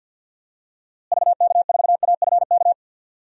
Morse code, heard as a keyed tone, about 51 wpm, 700 Hz. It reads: VK4AFK
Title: DXCC-100 Call Signs - 6 Characters - Repeated 3x - 50wpm